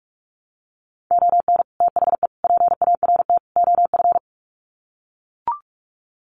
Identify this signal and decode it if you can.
Morse code, heard as a keyed tone, about 44 wpm, 700 Hz. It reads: ON THE PART OF
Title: Sets of 4 Words 45wpm